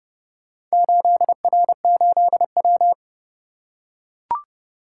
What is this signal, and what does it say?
Morse code, heard as a keyed tone, about 29 wpm, 700 Hz. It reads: 8R8W